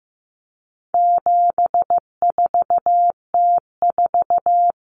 Morse code, heard as a keyed tone, about 15 wpm, 700 Hz. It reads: 74T4
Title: Common Cut Numbers - 4-Digit 15wpm